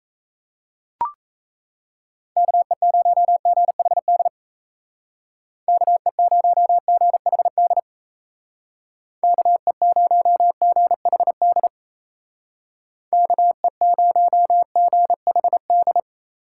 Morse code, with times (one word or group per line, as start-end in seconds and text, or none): 2.4-4.3 KE0GHD
5.7-7.8 KE0GHD
9.2-11.7 KE0GHD
13.1-16.0 KE0GHD